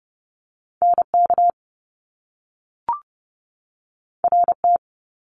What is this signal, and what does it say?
Morse code, heard as a keyed tone, about 30 wpm, 700 Hz. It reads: NK RT
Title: Top Two Letter Combinations in Words 30wpm